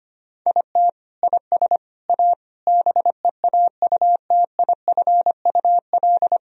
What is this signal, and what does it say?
Morse code, heard as a keyed tone, about 25 wpm, 700 Hz. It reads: IT IS A BEAUTIFUL
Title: Sentences from Top 1000 Words - Review Part 1 - 25wpm